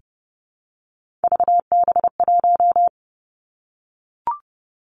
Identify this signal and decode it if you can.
Morse code, heard as a keyed tone, about 30 wpm, 700 Hz. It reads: VB1
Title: Three Challenging Letter-Number Combinations 30wpm